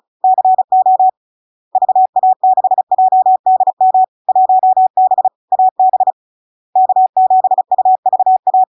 Morse code, heard as a keyed tone, about 36 wpm, 750 Hz. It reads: CO VA6JDM 1B AB K7UVA
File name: LARK-FD-2022-Log-in-CW-35wpm-750-Hz